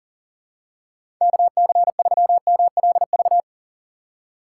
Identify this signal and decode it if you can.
Morse code, heard as a keyed tone, about 40 wpm, 700 Hz. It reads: KC3MLV
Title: USA Call Signs - 4x Character Spacing - 40wpm